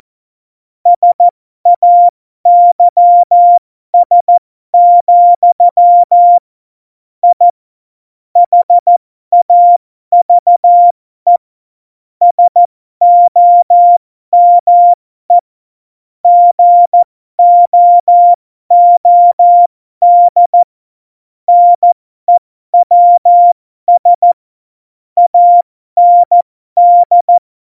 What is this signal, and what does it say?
Morse code, heard as a keyed tone, about 14 wpm, spaced as for 12 wpm, 700 Hz. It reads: SAYS, I HAVE SOME GOOD NEWS AND
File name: Morsekurs / Witze